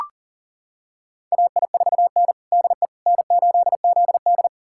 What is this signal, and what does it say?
Morse code, heard as a keyed tone, about 37 wpm, 700 Hz. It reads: AI4N DE N8ZD
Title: USA Callsign DE Callsign - 4 Characters - 40wpm